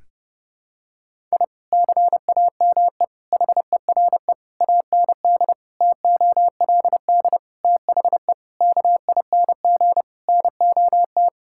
Morse code, heard as a keyed tone, about 30 wpm, 700 Hz. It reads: I CAME HERE AND TOLD THE KING NOT
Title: Sentences from Top 1000 Words - Review Part 6 - 30wpm